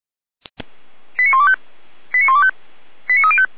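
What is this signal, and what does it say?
Five-tone selective call (ZVEI), heard as quick runs of five tones, tones of about 70 ms. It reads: 89125 89125 89305
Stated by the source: ZVEI five-tone call received by SDR.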